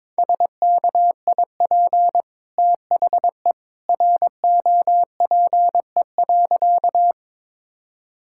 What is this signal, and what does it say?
Morse code, heard as a keyed tone, about 22 wpm, 700 Hz. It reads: SKIP THE ROPE.
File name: practice / 22wpm / Easy Sentences with Hints - 22wpm